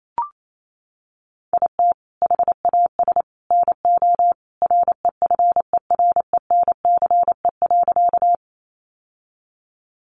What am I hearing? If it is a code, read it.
Morse code, heard as a keyed tone, about 28 wpm, 700 Hz. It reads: IT HAS NO REFERENCE.
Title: Sentences from Top 1200 Words 28wpm